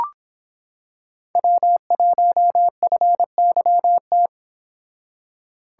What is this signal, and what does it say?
Morse code, heard as a keyed tone, about 26 wpm, 700 Hz. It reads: W1FYT